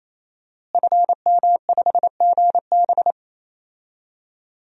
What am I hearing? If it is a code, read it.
Morse code, heard as a keyed tone, about 28 wpm, 700 Hz. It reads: FM5GB
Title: DXCC-100 Call Signs - 5 Characters - Repeated 3x - 28wpm